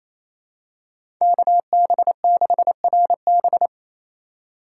Morse code, heard as a keyed tone, about 27 wpm, 700 Hz. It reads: KB6RB
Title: USA Call Signs - 5 Characters - 4x Character Spacing - 28wpm